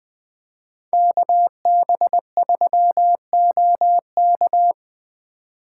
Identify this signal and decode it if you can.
Morse code, heard as a keyed tone, about 24 wpm, 700 Hz. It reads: KB3OK
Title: Speed Racing - USA 2x2 Call Signs 20wpm